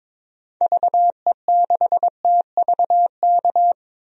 Morse code, heard as a keyed tone, about 22 wpm, 700 Hz. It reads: VE6TVK
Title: DXCC-100 Call Signs - 6 Characters - 22wpm